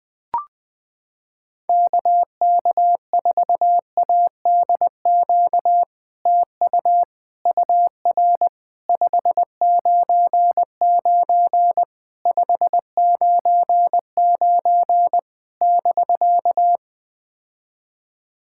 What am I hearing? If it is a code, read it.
Morse code, heard as a keyed tone, about 20 wpm, 700 Hz. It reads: KK4ADQ TU UR 599 599 <BK>
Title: POTA QSO Practice - 20wpm